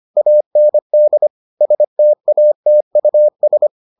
Morse code, heard as a keyed tone, about 25 wpm, 600 Hz. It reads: AND STATUS